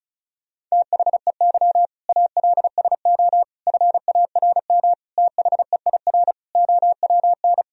Morse code, heard as a keyed tone, about 34 wpm, 700 Hz. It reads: THEY ALSO FARM THEIR OWN
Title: Sentences from Top 1000 Words - Review Part 2 - 35wpm